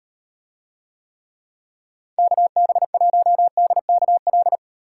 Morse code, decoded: KB1DKL

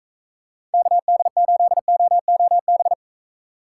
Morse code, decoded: KD8OOB

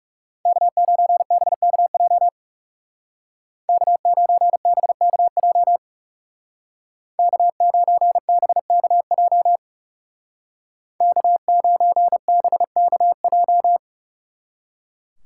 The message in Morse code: K9BKJ K9BKJ K9BKJ K9BKJ

